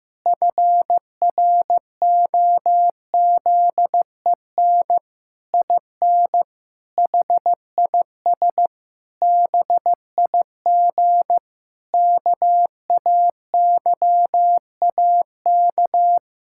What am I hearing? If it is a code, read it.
Morse code, heard as a keyed tone, about 15 wpm, 700 Hz. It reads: FROZEN IN HIS BIG KAYAK
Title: Pangram Challenge v2 - 15wpm